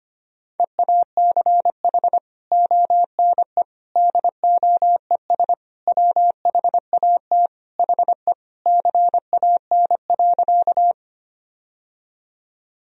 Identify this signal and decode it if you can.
Morse code, heard as a keyed tone, about 25 wpm, 700 Hz. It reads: EACH ONE DOES WHAT HE CAN.
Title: Sentences from Top 1000 Words - Review Part 1 - 25wpm